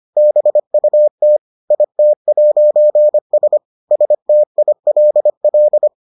Morse code, heard as a keyed tone, about 25 wpm, 600 Hz. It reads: BUT IT'S STILL